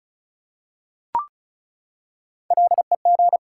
Morse code, heard as a keyed tone, about 36 wpm, 700 Hz. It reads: LEG
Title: Three Letter Words - 35wpm